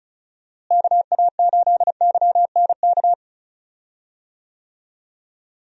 Morse code, heard as a keyed tone, about 34 wpm, 700 Hz. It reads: KA8YNK
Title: USA Call Signs - 6 Characters - Repeated 3x - 35wpm